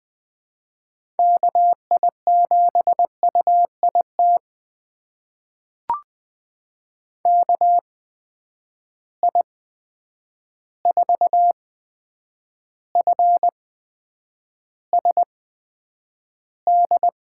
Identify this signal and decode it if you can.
Morse code, heard as a keyed tone, about 20 wpm, spaced as for 9 wpm, 700 Hz. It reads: KI7UIT KI4FSD
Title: USA Call Signs - 6 Characters - 8x Character Spacing - 20wpm